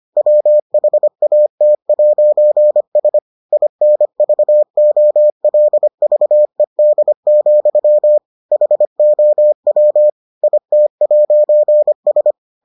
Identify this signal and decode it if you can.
Morse code, heard as a keyed tone, about 25 wpm, 600 Hz. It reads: WHAT'S INVOLVED, HOW IT'S